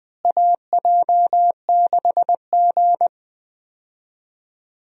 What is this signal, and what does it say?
Morse code, heard as a keyed tone, about 20 wpm, 700 Hz. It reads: AJ6G
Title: USA Call Signs - 4 Characters - Repeated 2x - 20wpm